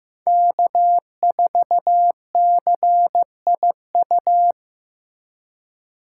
Morse code, heard as a keyed tone, about 18 wpm, 700 Hz. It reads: K4CIU
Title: Speed Racing - USA 1x3 Call Signs 15wpm